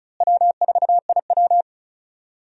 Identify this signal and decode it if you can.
Morse code, heard as a keyed tone, about 36 wpm, 700 Hz. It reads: W4IW
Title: USA Call Signs - 4 Characters - Repeated 2x - 35wpm